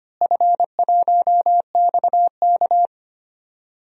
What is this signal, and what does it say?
Morse code, heard as a keyed tone, about 25 wpm, 700 Hz. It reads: F1XK
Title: DXCC-100 Call Signs - 4 Characters - Repeated 2x - 25wpm